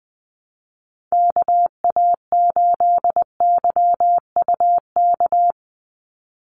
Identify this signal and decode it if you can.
Morse code, heard as a keyed tone, about 20 wpm, 700 Hz. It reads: KA8YUK
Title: USA Call Signs 20wpm